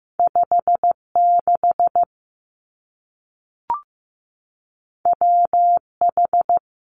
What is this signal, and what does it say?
Morse code, heard as a keyed tone, about 15 wpm, 700 Hz. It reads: 5 6 W H
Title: Two Challenging Letter-Number Combinations 15wpm